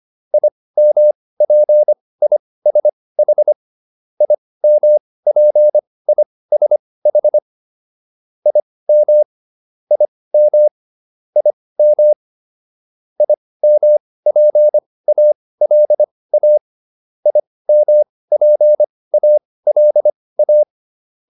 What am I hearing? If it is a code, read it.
Morse code, heard as a keyed tone, about 25 wpm, spaced as for 18 wpm, 600 Hz. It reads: IMPISH IMPISH IM IM IM IMPALA IMPALA